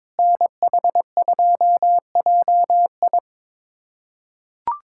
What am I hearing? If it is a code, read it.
Morse code, heard as a keyed tone, about 22 wpm, 700 Hz. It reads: NH2JI
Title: USA Callsign DE Callsign - 5 Characters - 22wpm